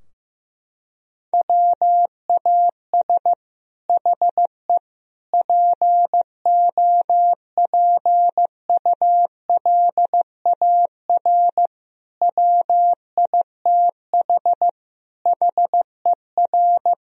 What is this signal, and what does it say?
Morse code, heard as a keyed tone, about 15 wpm, 700 Hz. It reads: WAS HE POPULAR WITH HER
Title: Sentences from Top 1000 Words - Review Part 3 - 15wpm